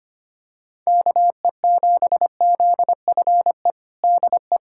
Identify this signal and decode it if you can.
Morse code, heard as a keyed tone, about 25 wpm, 700 Hz. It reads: KE7ZFE DE
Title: USA Callsign DE Callsign 25wpm